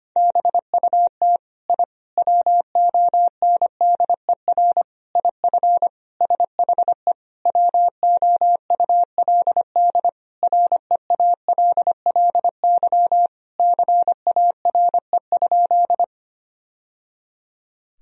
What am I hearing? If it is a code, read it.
Morse code, heard as a keyed tone, about 25 wpm, 700 Hz. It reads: BUT I WONDER IF SHE WOULD REALLY CARE?
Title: Sentences from Top 1000 Words - Review Part 3 - 25wpm